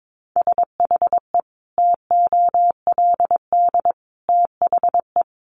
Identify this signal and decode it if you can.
Morse code, heard as a keyed tone, about 22 wpm, 700 Hz. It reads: SHE TOLD THE
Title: Sentences from Top 1500 Words 22wpm